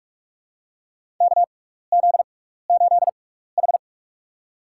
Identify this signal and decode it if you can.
Morse code, heard as a keyed tone, about 46 wpm, spaced as for 16 wpm, 700 Hz. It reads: KZ8H
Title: USA Call Signs - 6x Character Spacing - 45wpm